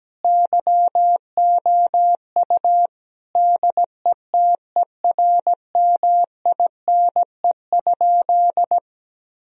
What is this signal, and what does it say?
Morse code, heard as a keyed tone, about 17 wpm, 700 Hz. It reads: YOU DETERMINE?
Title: Sentences from Top 1000 Words - Review Part 4 - 17wpm